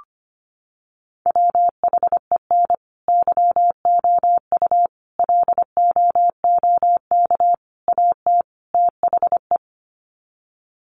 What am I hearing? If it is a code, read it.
Morse code, heard as a keyed tone, about 25 wpm, 700 Hz. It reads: WHEN YOU LOOK AT THE
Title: Sets of 5 Words 25wpm